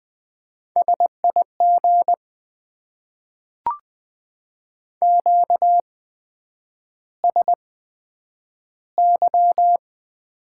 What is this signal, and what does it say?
Morse code, heard as a keyed tone, about 20 wpm, spaced as for 9 wpm, 700 Hz. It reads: SIG QSY
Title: Top 90 QSO Elements - 8x Character Spacing - 20wpm